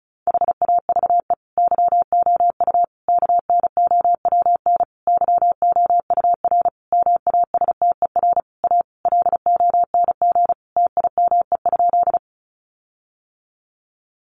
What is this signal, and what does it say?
Morse code, heard as a keyed tone, about 34 wpm, 700 Hz. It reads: HAVE YOU KNOWN YOUR MASTER A LONG TIME?
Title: Sentences from Top 1300 Words 35wpm